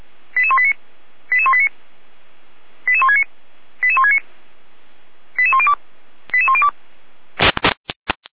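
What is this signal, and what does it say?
Five-tone selective call (ZVEI), heard as quick runs of five tones, tones of about 70 ms. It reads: 88189 88189 88169 88169 88102 88102
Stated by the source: ZVEI five-tone selective call 88166, received on an SDR.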